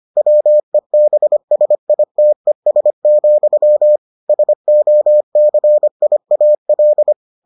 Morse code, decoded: WEBSITES, SOCIAL

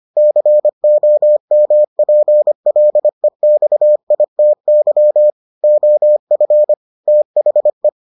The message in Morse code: COMPLEXITY OF THE

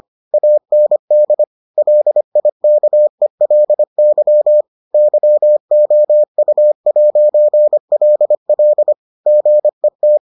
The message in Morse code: AND LIKELY YOU'LL GET

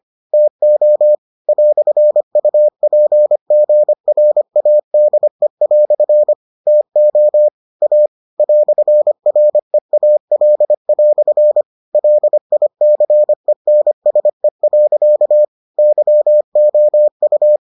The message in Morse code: TO "UPGRADE" TO A "REAL" LICENSE. YOU

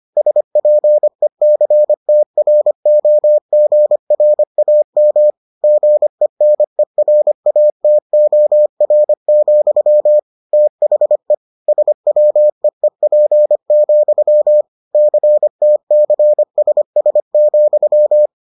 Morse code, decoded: SPECTROGRAM GENERATOR, THE SWEEP, CTCSS,

